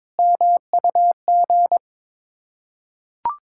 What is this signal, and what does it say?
Morse code, heard as a keyed tone, about 22 wpm, 690 Hz. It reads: MUG
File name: Difficult Words with Hints - 22wpm